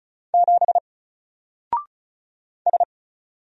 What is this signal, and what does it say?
Morse code, heard as a keyed tone, about 36 wpm, 700 Hz. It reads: 7 S
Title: Single Letter-Number Alt 35wpm